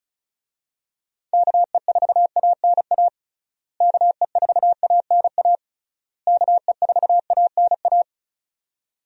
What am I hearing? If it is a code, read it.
Morse code, heard as a keyed tone, about 36 wpm, 700 Hz. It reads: KE4ANA KE4ANA KE4ANA
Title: USA Call Signs - 6 Characters - Repeated 3x - 35wpm